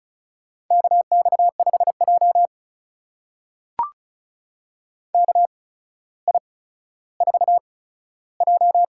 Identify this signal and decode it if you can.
Morse code, heard as a keyed tone, about 36 wpm, spaced as for 15 wpm, 700 Hz. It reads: KX5J KI4J
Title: USA Call Signs - 4 Characters - 8x Character Spacing - 35wpm